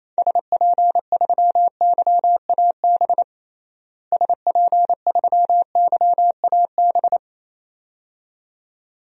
Morse code, heard as a keyed tone, about 28 wpm, 700 Hz. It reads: SP3YAB SP3YAB
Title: DXCC-100 Call Signs - 6 Characters - Repeated 2x - 28wpm